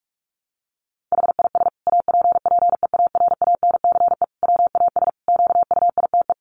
Morse code, heard as a keyed tone, about 44 wpm, 700 Hz. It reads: HIS APPEARANCE WAS QUITE
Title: Sentences from Top 1100 Words 45wpm